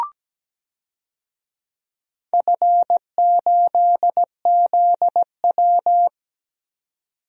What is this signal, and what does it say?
Morse code, heard as a keyed tone, about 17 wpm, 700 Hz. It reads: F8ZW
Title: DXCC-100 Call Signs - 4 Characters - 17wpm